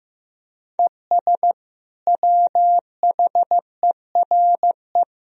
Morse code, heard as a keyed tone, about 16 wpm, 700 Hz. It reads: ES WHERE
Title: Sentences from Top 200 Words and Abbreviations 15wpm